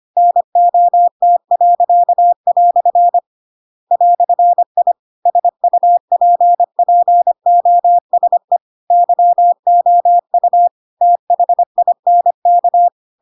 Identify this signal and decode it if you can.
Morse code, heard as a keyed tone, about 25 wpm, 700 Hz. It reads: NOT." "I SUPPOSE YOU THINK
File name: Pinocchio / Pinocchio ch2 25 25 ik1hge